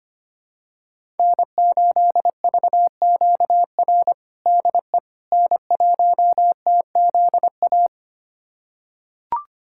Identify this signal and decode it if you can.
Morse code, heard as a keyed tone, about 25 wpm, 700 Hz. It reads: N8VQR DE N1TZA
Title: USA Callsign DE Callsign - 5 Characters - 25wpm